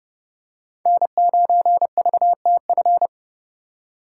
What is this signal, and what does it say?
Morse code, heard as a keyed tone, about 30 wpm, 700 Hz. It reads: N9VTF